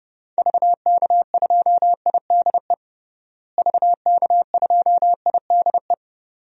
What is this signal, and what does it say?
Morse code, heard as a keyed tone, about 30 wpm, 700 Hz. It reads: VK2IDE VK2IDE
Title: DXCC-100 Call Signs - 6 Characters - Repeated 2x - 30wpm